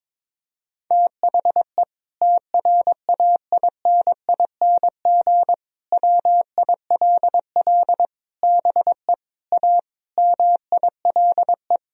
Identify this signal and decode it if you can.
Morse code, heard as a keyed tone, about 21 wpm, 700 Hz. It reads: THE TRAINING WILL BE A MILE
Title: Sentences from Top 1000 Words - Review Part 4 - 22wpm